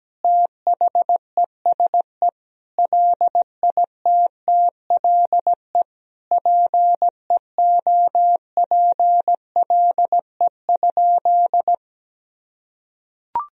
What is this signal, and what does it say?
Morse code, heard as a keyed tone, about 17 wpm, 700 Hz. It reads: THESE LITTLE PEOPLE?
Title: Sentences from Top 1000 Words - Review Part 1 - 17wpm